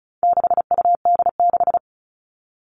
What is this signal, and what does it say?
Morse code, heard as a keyed tone, about 34 wpm, 700 Hz. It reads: 6UD6
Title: All Cut Numbers - 4-Digit 35wpm